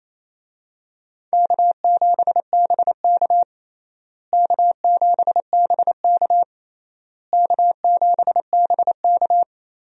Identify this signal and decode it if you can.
Morse code, heard as a keyed tone, about 28 wpm, 700 Hz. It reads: K7BK K7BK K7BK
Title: USA Call Signs - 4 Characters - Repeated 3x - 28wpm